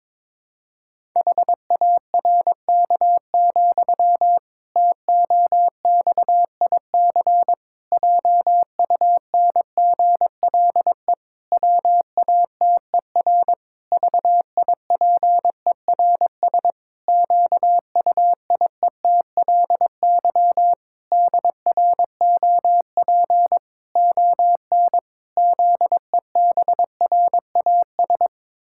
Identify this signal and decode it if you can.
Morse code, heard as a keyed tone, about 22 wpm, 700 Hz. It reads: HARK, TOXIC JUNGLE WATER VIPERS QUIETLY DROP ON ZEBRAS
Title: Pangram Challenge v2 - 22wpm